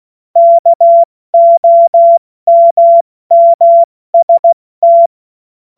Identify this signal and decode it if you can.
Morse code, heard as a keyed tone, about 16 wpm, 680 Hz. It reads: KOMMST
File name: Morsekurs / Witze / Witz 14. September 2021, 70 F80